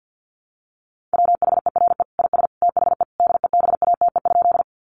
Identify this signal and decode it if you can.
Morse code, heard as a keyed tone, about 50 wpm, 700 Hz. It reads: WHERE IS THE DEBATE?
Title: Sentences from Top 1100 Words 50wpm